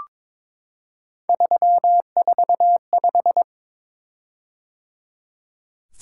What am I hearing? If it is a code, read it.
Morse code, heard as a keyed tone, about 22 wpm, 700 Hz. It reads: 345
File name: practice / 22wpm / Three Letter-Number Combinations - 22wpm